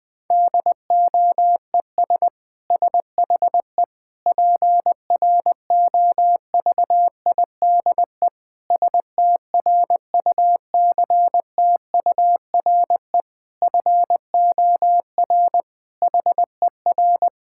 Morse code, heard as a keyed tone, about 20 wpm, 700 Hz. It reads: DOES SHE PROVIDE STRUCTURE FOR HER
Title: Sentences from Top 1000 Words - Review Part 6 - 20wpm